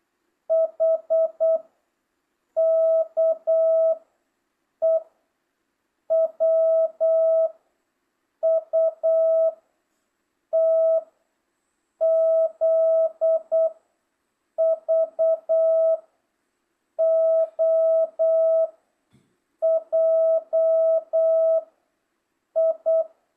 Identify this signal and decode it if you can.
Morse code, heard as a keyed tone, about 8 wpm, 650 Hz. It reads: H K E W U T Z V O J I